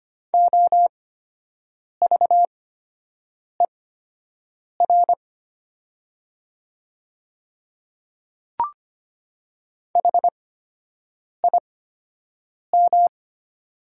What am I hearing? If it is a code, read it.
Morse code, heard as a keyed tone, about 25 wpm, spaced as for 7 wpm, 700 Hz. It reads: OVER HIM
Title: 8x Character Spacing - Top 100 Words - 25wpm